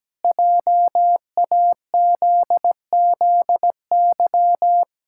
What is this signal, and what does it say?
Morse code, heard as a keyed tone, about 17 wpm, 700 Hz. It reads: JAZZY
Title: Pangram Challenge v2 - 17wpm